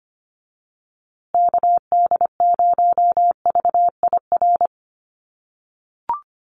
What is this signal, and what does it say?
Morse code, heard as a keyed tone, about 25 wpm, 700 Hz. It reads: KD0VIR
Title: USA Call Signs Repeated 3x 25wpm